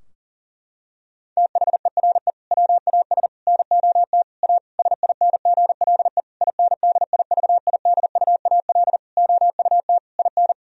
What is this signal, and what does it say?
Morse code, heard as a keyed tone, about 37 wpm, 700 Hz. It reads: THERE WAS NOT A SINGLE INDIVIDUAL OUT IN
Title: Sentences from Top 1000 Words - Review Part 4 - 40wpm